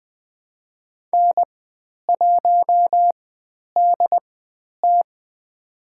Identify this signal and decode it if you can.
Morse code, heard as a keyed tone, about 20 wpm, spaced as for 10 wpm, 700 Hz. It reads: N1DT
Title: 102 Lesson 17g - Letter u - Call Signs 20wpm